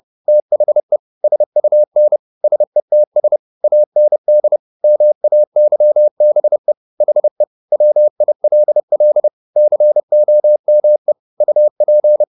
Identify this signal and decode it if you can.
Morse code, heard as a keyed tone, about 30 wpm, 600 Hz. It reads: THE SUN SETS AND MAYBE HE WILL COME UP
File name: the old man and the sea / the old man and the sea 1 30wpm